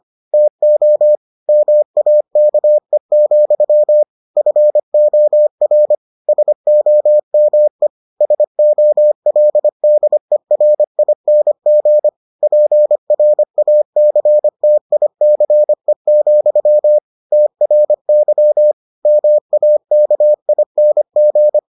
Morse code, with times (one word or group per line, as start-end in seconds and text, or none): 0.3-1.1 TO
1.5-4.0 MAKE,
4.4-6.0 FOR
6.3-7.9 SOME
8.2-12.1 SOLDERING
12.4-17.0 PRACTICE,
17.3-18.7 TRY
19.1-21.6 MAKING